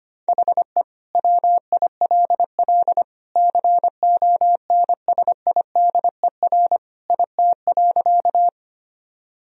Morse code, heard as a keyed tone, about 25 wpm, 700 Hz. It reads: HE WILL CONSIDER IT.
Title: Sentences from Top 1000 Words - Review Part 4 - 25wpm